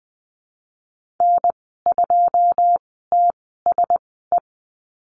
Morse code, heard as a keyed tone, about 20 wpm, spaced as for 15 wpm, 700 Hz. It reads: N2TSE
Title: 082 Lesson 14g - Number 2 - Call Signs 20wpm